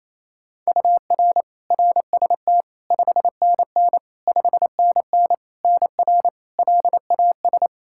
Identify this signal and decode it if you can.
Morse code, heard as a keyed tone, about 27 wpm, 700 Hz. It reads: UR RST 5NN 5NN NR LAS
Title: K1USN SST QSO Practice 28wpm